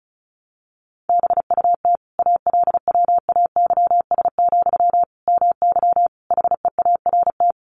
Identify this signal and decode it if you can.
Morse code, heard as a keyed tone, about 34 wpm, 700 Hz. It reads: BUT ALWAYS, MY HEART
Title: Sentences from Top 1200 Words 35wpm